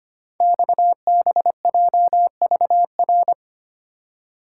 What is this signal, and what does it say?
Morse code, heard as a keyed tone, about 25 wpm, 700 Hz. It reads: XBJVR
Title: Code Groups - 5 Letters - 25wpm